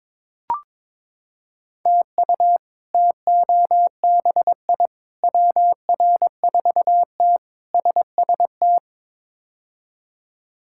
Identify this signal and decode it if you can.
Morse code, heard as a keyed tone, about 22 wpm, 700 Hz. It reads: TU TOBI WR4T SST
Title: K1USN SST QSO Practice - 22wpm